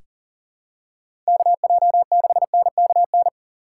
Morse code, heard as a keyed tone, about 40 wpm, 700 Hz. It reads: KJ6NKN
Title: USA Call Signs - 6 Characters - 3x Character Spacing - 40wpm